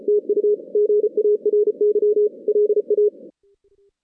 Morse code, heard as a keyed tone, about 34 wpm, 420 Hz. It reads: TU GARY LA
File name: 2016 JAN CW NAQP / 2016 Jan naqp k5ot - 0-10